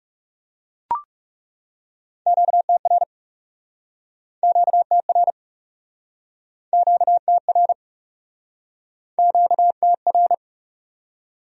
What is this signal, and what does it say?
Morse code, heard as a keyed tone, about 36 wpm, 700 Hz. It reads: QTR QTR QTR QTR
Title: Speed-Racing Q-Codes - 30wpm